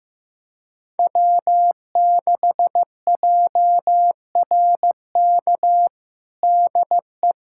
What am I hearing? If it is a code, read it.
Morse code, heard as a keyed tone, about 15 wpm, 700 Hz. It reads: W6JRK DE
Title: USA Callsign DE Callsign - 5 Characters - 15wpm